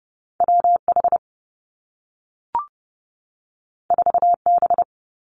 Morse code, heard as a keyed tone, about 30 wpm, 700 Hz. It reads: W H 4 B
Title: Two Challenging Letter-Number Combinations 30wpm